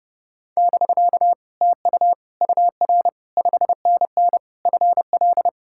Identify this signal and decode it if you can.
Morse code, heard as a keyed tone, about 30 wpm, 700 Hz. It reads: <BK> TU UR 5NN FL